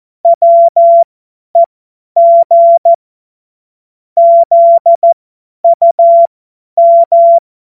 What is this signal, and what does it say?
Morse code, heard as a keyed tone, about 14 wpm, spaced as for 10 wpm, 680 Hz. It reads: WEG ZUM